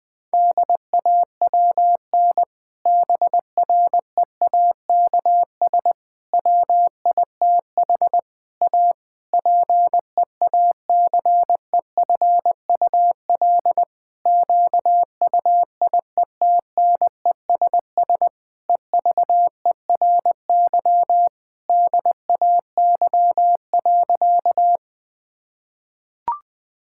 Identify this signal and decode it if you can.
Morse code, heard as a keyed tone, about 21 wpm, 700 Hz. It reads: DAWN BREAKS WITH A PEACEFUL QUIETNESS EVERY DAY.
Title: Difficult Sentences with Hints - 20wpm